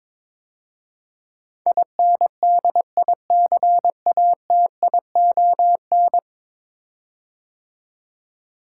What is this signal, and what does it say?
Morse code, heard as a keyed tone, about 23 wpm, 700 Hz. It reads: INDICATION
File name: practice / 22wpm / 10 Letter Words - First 3 Letters Given - 22wpm